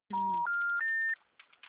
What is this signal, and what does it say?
Special information tone.
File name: out- -2007